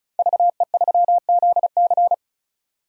Morse code, heard as a keyed tone, about 34 wpm, 700 Hz. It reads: VE3ZC
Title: DXCC-100 Call Signs - 5 Characters - 35wpm